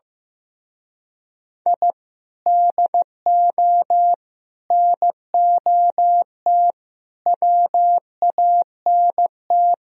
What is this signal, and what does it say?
Morse code, heard as a keyed tone, about 17 wpm, 700 Hz. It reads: I DO NOT WANT
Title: Speed-Racing - Sets of 5 Words 15wpm